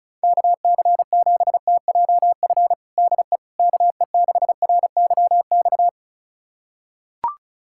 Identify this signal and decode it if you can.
Morse code, heard as a keyed tone, about 36 wpm, 700 Hz. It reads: KC7TJF DE KE6RYX
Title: USA Callsign DE Callsign - 6 Characters - 35wpm